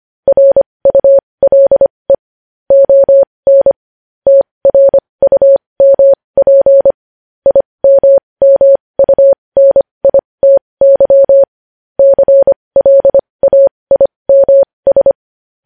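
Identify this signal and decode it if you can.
Morse code, heard as a keyed tone, about 25 wpm, spaced as for 20 wpm, 550 Hz. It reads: RULE ON TRUMP IMMUNITY CLAIMS
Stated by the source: Morse code practice sending news headlines at 20 wpm.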